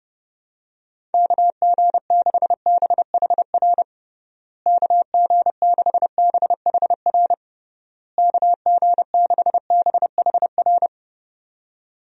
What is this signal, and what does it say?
Morse code, heard as a keyed tone, about 30 wpm, 700 Hz. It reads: KG6BHR KG6BHR KG6BHR
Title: USA Call Signs - 6 Characters - Repeated 3x - 30wpm